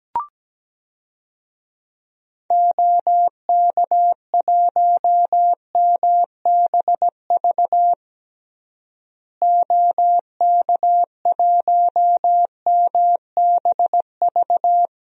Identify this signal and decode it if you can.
Morse code, heard as a keyed tone, about 17 wpm, 700 Hz. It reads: OK1MBV OK1MBV
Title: DXCC-100 Call Signs - 6 Characters - Repeated 3x - 17wpm